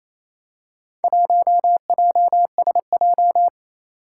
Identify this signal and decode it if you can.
Morse code, heard as a keyed tone, about 27 wpm, 700 Hz. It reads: 1JSJ
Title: Four Challenging Letter-Number Combinations 28wpm